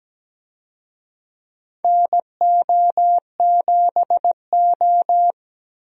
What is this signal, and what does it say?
Morse code, heard as a keyed tone, about 17 wpm, 700 Hz. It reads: NO7O
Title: USA Call Signs - 4 Characters - 17wpm